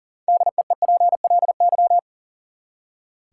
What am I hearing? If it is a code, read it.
Morse code, heard as a keyed tone, about 40 wpm, 700 Hz. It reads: DEEPLY
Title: Six Letter Words - 40wpm